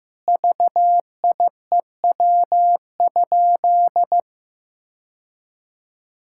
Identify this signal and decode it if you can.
Morse code, heard as a keyed tone, about 15 wpm, 700 Hz. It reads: VIEW?
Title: Sentences from Top 1000 Words - Review Part 6 - 15wpm